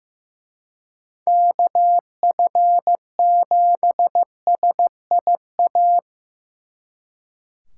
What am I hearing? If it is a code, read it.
Morse code, heard as a keyed tone, about 18 wpm, 700 Hz. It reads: KF7SIA
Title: Speed Racing - USA 2x3 Call Signs 15wpm